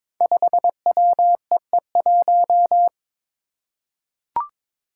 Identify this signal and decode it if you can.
Morse code, heard as a keyed tone, about 22 wpm, 700 Hz. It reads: 5WEE1
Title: Five Challenging Letter-Number Combinations 22wpm